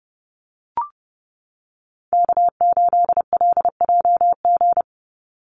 Morse code, heard as a keyed tone, about 30 wpm, 700 Hz. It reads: K8LJG
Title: NA SOTA Chaser Callsigns 30wpm